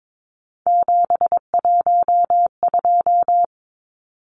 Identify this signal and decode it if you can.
Morse code, heard as a keyed tone, about 22 wpm, 700 Hz. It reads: 712
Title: Numbers - 3-Digit 22wpm